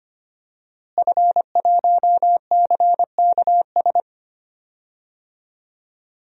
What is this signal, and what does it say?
Morse code, heard as a keyed tone, about 25 wpm, 700 Hz. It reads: F1CKS
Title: DXCC-100 Call Signs - 5 Characters - Repeated 3x - 25wpm